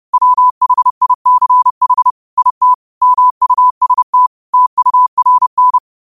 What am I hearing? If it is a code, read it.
Morse code, heard as a keyed tone, about 30 wpm, 1000 Hz. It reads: WHICH IT MUST TURN